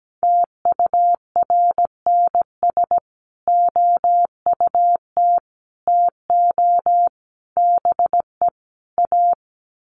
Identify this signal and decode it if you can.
Morse code, heard as a keyed tone, about 17 wpm, 700 Hz. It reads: TURNS OUT TO BE A
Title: Sets of 5 Words Encore 17wpm